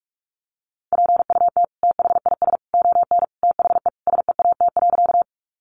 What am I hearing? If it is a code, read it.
Morse code, heard as a keyed tone, about 44 wpm, 700 Hz. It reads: PUT THIS ON THE SEAT.
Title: Sentences from Top 1500 Words 45wpm